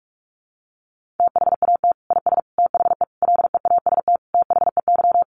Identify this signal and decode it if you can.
Morse code, heard as a keyed tone, about 44 wpm, 700 Hz. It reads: THAT IS THE LEAST THEY
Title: Sentences from Top 700 Words 45wpm